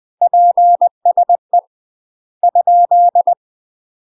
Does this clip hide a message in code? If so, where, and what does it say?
Yes: Morse code from 0.2 to 3.3 s, heard as a keyed tone, about 20 wpm, 700 Hz. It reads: PSE ?